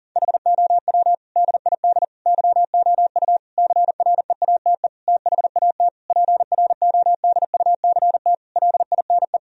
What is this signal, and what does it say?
Morse code, heard as a keyed tone, about 37 wpm, 700 Hz. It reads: HOW DID YOU CREATE THAT PRODUCT LINE
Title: Sentences from Top 1000 Words - Review Part 3 - 40wpm